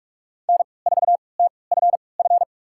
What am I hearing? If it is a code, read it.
Morse code, heard as a keyed tone, about 42 wpm, spaced as for 25 wpm, 700 Hz. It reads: N4TFF
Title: USA Call Signs - 3x Character Spacing - 45wpm